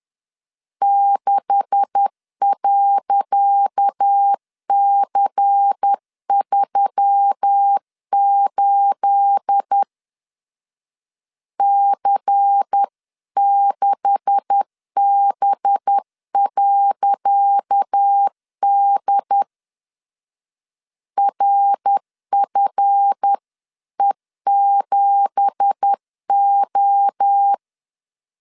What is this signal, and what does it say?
Morse code, heard as a keyed tone, about 10 wpm, 800 Hz. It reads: 6.C38 C6B.D RFE7O
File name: LEZIONI / 10 SIGNS / 10C DOT 10wpm 2940